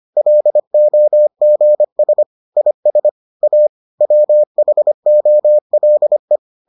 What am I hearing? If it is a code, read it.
Morse code, heard as a keyed tone, about 25 wpm, 600 Hz. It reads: LOGS IS A WHOLE